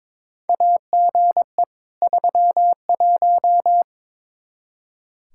Morse code, heard as a keyed tone, about 22 wpm, 700 Hz. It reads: AGE 31
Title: Practice QSO - Exchange 3 - More Info - 22wpm